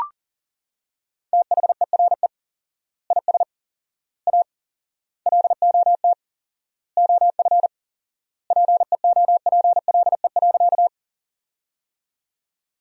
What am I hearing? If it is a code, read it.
Morse code, heard as a keyed tone, about 40 wpm, 700 Hz. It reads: THERE IS A LOT OF PEOPLE.
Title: 4x Word-Spacing - Sentences from Top 500 Words - 40wpm